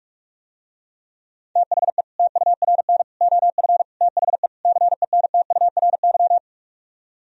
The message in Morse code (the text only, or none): THE TURN OF THE CENTURY